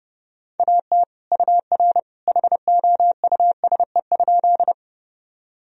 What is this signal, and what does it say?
Morse code, heard as a keyed tone, about 30 wpm, 700 Hz. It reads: AT UR HOUSE?
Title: Sentences from Top 200 Words and Abbreviations 30wpm